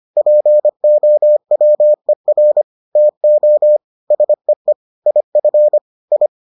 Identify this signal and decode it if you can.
Morse code, heard as a keyed tone, about 25 wpm, 600 Hz. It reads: POWER TO SEE IF I